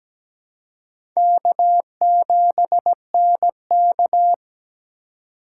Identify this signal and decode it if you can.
Morse code, heard as a keyed tone, about 17 wpm, 700 Hz. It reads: K7NK
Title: USA Call Signs - 4 Characters - 3x Character Spacing - 17wpm